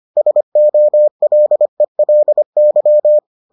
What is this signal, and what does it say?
Morse code, heard as a keyed tone, about 25 wpm, 600 Hz. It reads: SOLELY